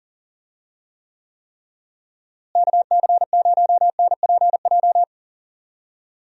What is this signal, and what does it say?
Morse code, heard as a keyed tone, about 40 wpm, 700 Hz. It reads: KC0NPJ
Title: USA Call Signs - 6 Characters - Repeated 3x - 40wpm